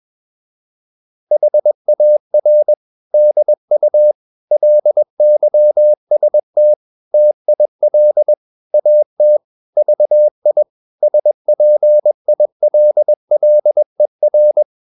Morse code, heard as a keyed tone, about 21 wpm, 600 Hz. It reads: HAR DU LYST TIL AT VI SPILLER
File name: Ferie QRS 21wpm